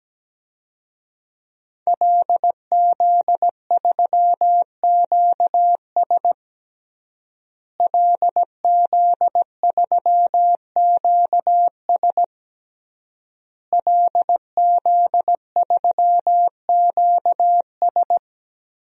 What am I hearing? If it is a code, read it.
Morse code, heard as a keyed tone, about 17 wpm, 700 Hz. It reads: LZ3QS LZ3QS LZ3QS
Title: DXCC-100 Call Signs - 5 Characters - Repeated 3x - 17wpm